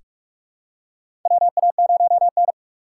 Morse code, heard as a keyed tone, about 44 wpm, 700 Hz. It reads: WA0N
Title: USA Call Signs - 3x Character Spacing - 45wpm